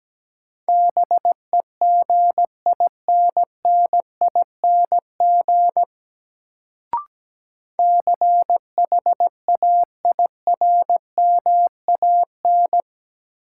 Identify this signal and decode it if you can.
Morse code, heard as a keyed tone, about 17 wpm, 700 Hz. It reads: BEGINNING CHAIRMAN
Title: Sets of 3 Words - New Word Drill - 17wpm